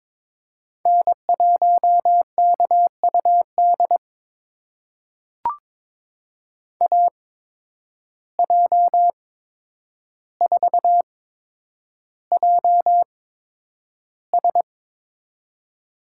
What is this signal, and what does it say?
Morse code, heard as a keyed tone, about 22 wpm, spaced as for 9 wpm, 700 Hz. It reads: N1KUD AJ4JS
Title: USA Call Signs - 5 Characters - 8x Character Spacing - 22wpm